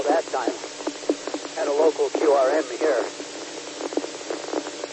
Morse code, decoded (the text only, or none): none